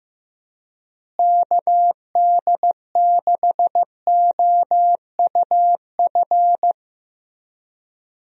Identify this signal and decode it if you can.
Morse code, heard as a keyed tone, about 15 wpm, 700 Hz. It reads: KD6OUF
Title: USA Call Signs - 6 Characters - Repeated 3x - 15wpm